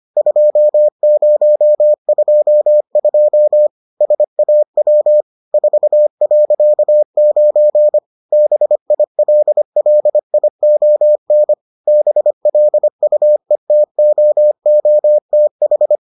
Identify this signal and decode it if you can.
Morse code, heard as a keyed tone, about 25 wpm, 600 Hz. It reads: 2022 SAW 4.9 BILLION BLUETOOTH